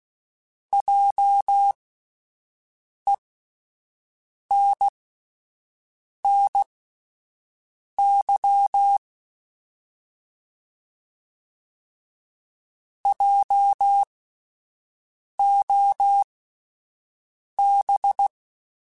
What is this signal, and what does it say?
Morse code, heard as a keyed tone, about 16 wpm, spaced as for 5 wpm, 800 Hz. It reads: JENNY JOB